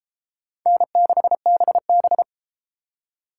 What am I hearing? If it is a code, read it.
Morse code, heard as a keyed tone, about 30 wpm, 700 Hz. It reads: N6BB